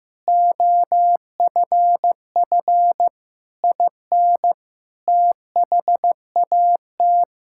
Morse code, heard as a keyed tone, about 15 wpm, 700 Hz. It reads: OFF IN THAT
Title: Sentences from Top 1000 Words - Review Part 5 - 15wpm